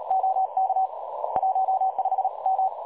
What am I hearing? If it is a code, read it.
Morse code, heard as a keyed tone, about 38 wpm, 760 Hz. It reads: OK 15M
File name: dlazaawansowanych / nagraniazpasma / websdr recording 3565.0kHz